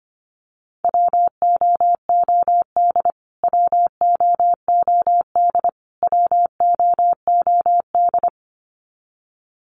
Morse code, heard as a keyed tone, about 25 wpm, 700 Hz. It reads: WOOD WOOD WOOD
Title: Top 500 Words Repeated 25wpm